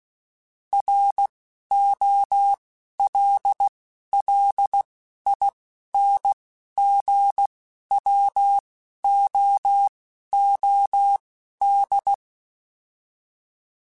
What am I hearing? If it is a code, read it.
Morse code, heard as a keyed tone, about 16 wpm, spaced as for 11 wpm, 800 Hz. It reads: ROLLINGWOOD